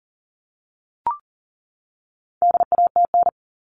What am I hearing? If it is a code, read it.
Morse code, heard as a keyed tone, about 40 wpm, 700 Hz. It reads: DATN
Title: All Cut Numbers - 4-Digit 40wpm